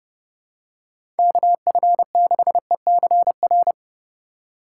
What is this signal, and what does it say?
Morse code, heard as a keyed tone, about 30 wpm, 690 Hz. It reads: KF6ECR
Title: USA Call Signs - 4x Character Spacing - 30wpm